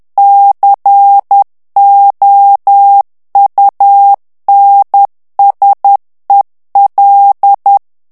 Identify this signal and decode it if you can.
Morse code, heard as a keyed tone, about 10 wpm, 800 Hz. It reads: COUNSEL
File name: ANGLAIS texte clair 3 vitesse 10 mots minute